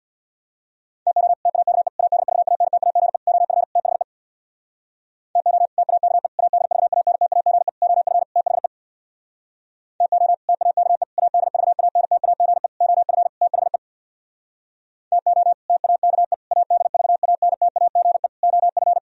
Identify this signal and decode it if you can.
Morse code, heard as a keyed tone, about 64 wpm, 700 Hz. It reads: TO TAKE ADVANTAGE OF THE TO TAKE ADVANTAGE OF THE TO TAKE ADVANTAGE OF THE TO TAKE ADVANTAGE OF